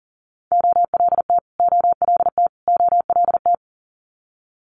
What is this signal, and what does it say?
Morse code, heard as a keyed tone, about 38 wpm, 700 Hz. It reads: OLT OLT OLT
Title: Multi-letter Phonemes Repeated 3x 40wpm